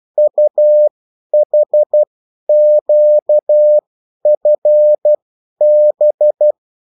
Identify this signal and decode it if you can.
Morse code, heard as a keyed tone, about 12 wpm, spaced as for 10 wpm, 600 Hz. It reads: UHQFB
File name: morse mp3 / revision9-vitesse 10 mots